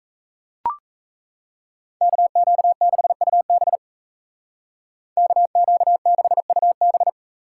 Morse code, heard as a keyed tone, about 36 wpm, 700 Hz. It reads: KQ6UB KQ6UB